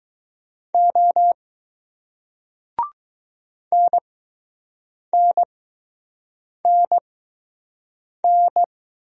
Morse code, heard as a keyed tone, about 20 wpm, spaced as for 6 wpm, 700 Hz. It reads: O NNNN